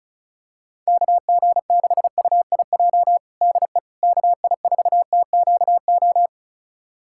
Morse code, heard as a keyed tone, about 36 wpm, 700 Hz. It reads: KG6UIJ DE KI4TQO
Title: USA Callsign DE Callsign - 6 Characters - 35wpm